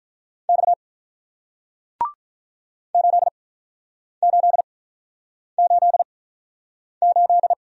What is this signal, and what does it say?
Morse code, heard as a keyed tone, about 48 wpm, spaced as for 9 wpm, 700 Hz. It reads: X 8888